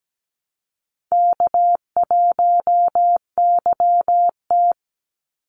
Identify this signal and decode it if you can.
Morse code, heard as a keyed tone, about 17 wpm, 700 Hz. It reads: K1YT
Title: Easy Call Signs Repeated 3x - 17wpm